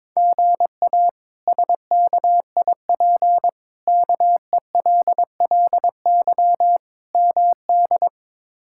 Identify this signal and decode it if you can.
Morse code, heard as a keyed tone, about 22 wpm, 700 Hz. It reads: GA SKIP KELLY MD
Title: K1USN SST QSO Practice - 22wpm